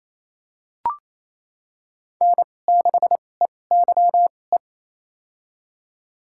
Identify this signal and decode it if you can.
Morse code, heard as a keyed tone, about 28 wpm, spaced as for 20 wpm, 700 Hz. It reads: N6EYE
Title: USA Call Signs - 5 Characters - 2x Character Spacing - 28wpm